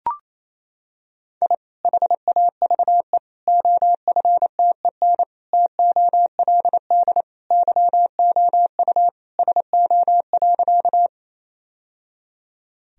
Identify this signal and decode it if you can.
Morse code, heard as a keyed tone, about 28 wpm, 700 Hz. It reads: I HAVE OFTEN TOLD YOU SO.
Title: Sentences from Top 1000 Words - Review Part 1 - 28wpm